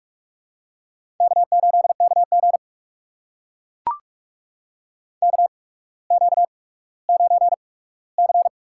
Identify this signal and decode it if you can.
Morse code, heard as a keyed tone, about 44 wpm, spaced as for 19 wpm, 700 Hz. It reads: K8KG KQ9C
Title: USA Call Signs - 4 Characters - 8x Character Spacing - 45wpm